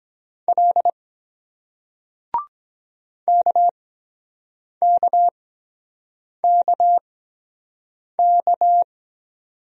Morse code, heard as a keyed tone, about 23 wpm, 700 Hz. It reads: L K K K K